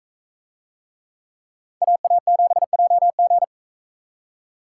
Morse code, heard as a keyed tone, about 44 wpm, 700 Hz. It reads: AA7JG